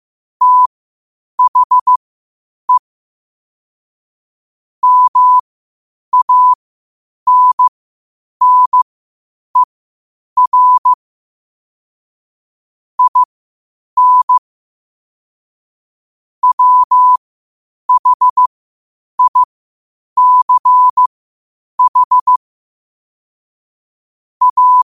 Morse code, heard as a keyed tone, about 15 wpm, spaced as for 8 wpm, 1000 Hz. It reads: THE MANNER IN WHICH A
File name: qotdfiles / 8wpm